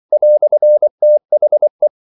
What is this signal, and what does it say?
Morse code, heard as a keyed tone, about 24 wpm, 600 Hz. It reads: "THE
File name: CW / Practice / EnglishQuotations0324